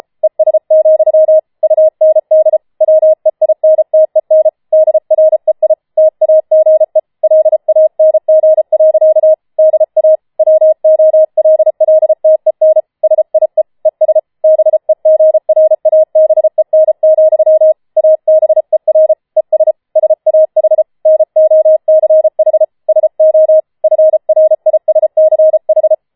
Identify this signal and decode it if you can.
Morse code, heard as a keyed tone, about 33 wpm, 630 Hz. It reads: ES, UND WEINTEN DREI TAGE LANG. DA WOLLTEN SIE ES BEGRABEN, ABER ES SAH NOCH SO FRISCH
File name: Schneewittchen Teil5 34wpm